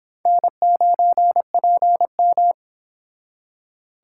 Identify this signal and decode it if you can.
Morse code, heard as a keyed tone, about 25 wpm, 700 Hz. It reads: N9PM